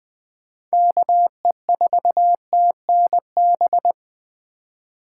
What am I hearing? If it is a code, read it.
Morse code, heard as a keyed tone, about 20 wpm, 700 Hz. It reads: KE4TNB
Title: USA Call Signs - 6 Characters - 2x Character Spacing - 20wpm